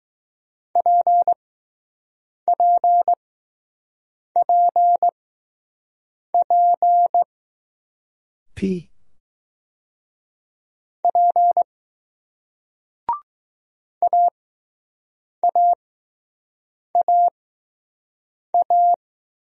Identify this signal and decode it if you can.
Morse code, heard as a keyed tone, about 21 wpm, spaced as for 6 wpm, 700 Hz. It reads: PPPP P AAAA